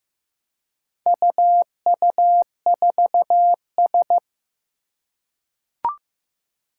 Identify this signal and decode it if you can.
Morse code, heard as a keyed tone, about 15 wpm, 700 Hz. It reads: UU4S
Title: Four Challenging Letter-Number Combinations 15wpm